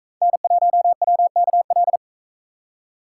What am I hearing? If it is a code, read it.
Morse code, heard as a keyed tone, about 44 wpm, 700 Hz. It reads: N1WKL